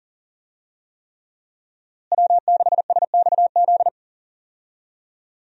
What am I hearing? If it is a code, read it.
Morse code, heard as a keyed tone, about 39 wpm, 700 Hz. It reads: W6SXZ
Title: USA Call Signs - 5 Characters - Repeated 3x - 40wpm